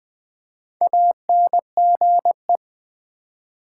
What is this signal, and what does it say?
Morse code, heard as a keyed tone, about 20 wpm, 700 Hz. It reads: ANGE
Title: Build A Word - 1 Letter At A Time - 5-Letter Words - 20wpm